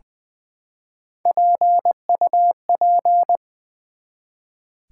Morse code, heard as a keyed tone, about 20 wpm, 700 Hz. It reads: PUP
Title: Build A Word - 1 Letter At A Time - 5-Letter Words - 20wpm